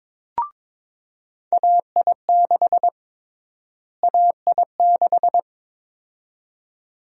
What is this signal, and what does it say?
Morse code, heard as a keyed tone, about 22 wpm, 700 Hz. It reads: AI6 AI6
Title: USA Call Sign Prefixes - Repeated 2x - 22wpm